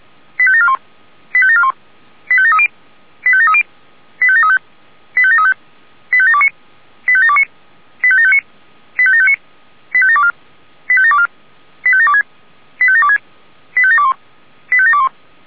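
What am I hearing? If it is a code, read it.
Five-tone selective call (ZVEI), heard as quick runs of five tones, tones of about 70 ms. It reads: 85741 85741 85730 85730 85735 85735 85729 85729 85759 85759 85724 85724 85726 85726 85721 85721
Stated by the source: ZVEI five-tone selective call received by SDR.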